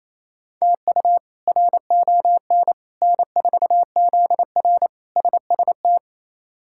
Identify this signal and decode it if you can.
Morse code, heard as a keyed tone, about 28 wpm, 700 Hz. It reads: TU RON N4ZR SST
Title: K1USN SST QSO Practice - 28wpm